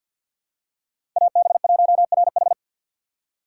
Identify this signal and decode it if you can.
Morse code, heard as a keyed tone, about 47 wpm, 700 Hz. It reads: AB1RH
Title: USA Call Signs - 5 Characters - 6x Character Spacing - 50wpm